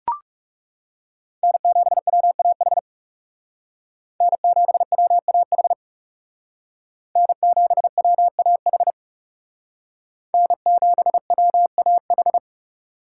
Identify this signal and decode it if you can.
Morse code, heard as a keyed tone, about 38 wpm, 700 Hz. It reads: N7WAH N7WAH N7WAH N7WAH